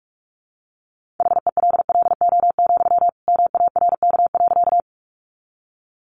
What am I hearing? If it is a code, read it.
Morse code, heard as a keyed tone, about 44 wpm, 700 Hz. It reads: HELLO, MARK.
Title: Sentences from Top 1100 Words 45wpm